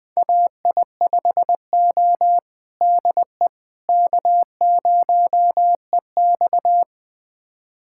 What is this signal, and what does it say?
Morse code, heard as a keyed tone, about 20 wpm, 700 Hz. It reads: AI5O DE K0EX
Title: USA Callsign DE Callsign - 4 Characters - 20wpm